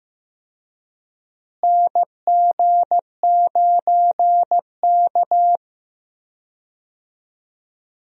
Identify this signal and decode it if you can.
Morse code, heard as a keyed tone, about 15 wpm, 700 Hz. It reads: NG9K
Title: USA Call Signs - 4 Characters - Repeated 2x - 15wpm